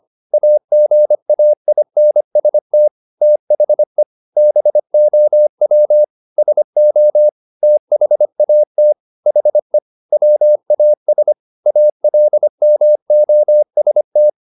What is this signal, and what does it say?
Morse code, heard as a keyed tone, about 25 wpm, 600 Hz. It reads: AGAINST THE BOW SO THAT HE WAS ALMOST
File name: the old man and the sea part 1 25wpm